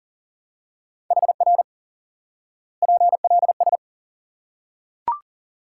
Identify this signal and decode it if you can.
Morse code, heard as a keyed tone, about 40 wpm, 700 Hz. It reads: HR PLS
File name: IQE Course 3 - Lesson 89 - YAGI - Two Elements - 40wpm